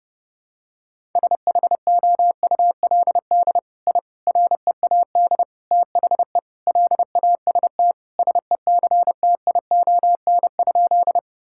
Morse code, heard as a keyed tone, about 30 wpm, 700 Hz. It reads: SHOULD I READ THE LAST SECTION?
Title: Sentences from Top 1000 Words - Review Part 5 - 30wpm